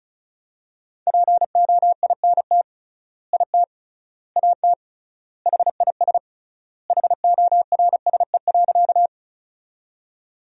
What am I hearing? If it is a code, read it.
Morse code, heard as a keyed tone, about 36 wpm, 700 Hz. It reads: POINT IT AT HIS HORSE.
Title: 3x Word-Spacing - Sentences from Top 300 Words - 35wpm